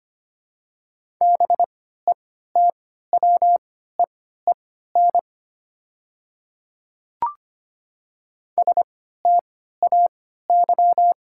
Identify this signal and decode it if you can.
Morse code, heard as a keyed tone, about 25 wpm, spaced as for 14 wpm, 700 Hz. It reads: BETWEEN STAY
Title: 3x Character Spacing - Top 500 Words - 25wpm